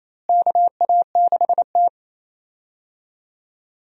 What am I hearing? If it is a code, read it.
Morse code, heard as a keyed tone, about 28 wpm, 700 Hz. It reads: KA6T
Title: USA Call Signs - 4 Characters - Repeated 2x - 28wpm